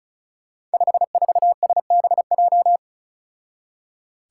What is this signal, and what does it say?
Morse code, heard as a keyed tone, about 36 wpm, 700 Hz. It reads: 54SBJ
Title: Five Challenging Letter-Number Combinations 35wpm